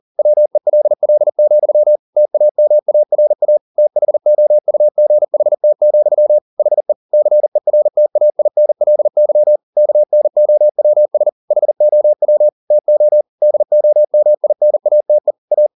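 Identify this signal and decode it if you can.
Morse code, heard as a keyed tone, about 40 wpm, 600 Hz. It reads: WELL, TAMARA THOUGHT, HE CERTAINLY KNOWS HOW TO DOMINATE A
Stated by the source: Morse code at 40 wpm, practice text, CW med Gnister.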